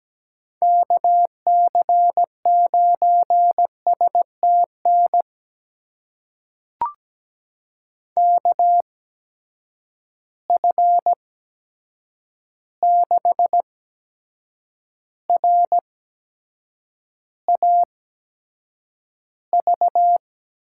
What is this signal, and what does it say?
Morse code, heard as a keyed tone, about 17 wpm, spaced as for 7 wpm, 700 Hz. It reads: KC9STN KF6RAV
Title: USA Call Signs - 6 Characters - 8x Character Spacing - 17wpm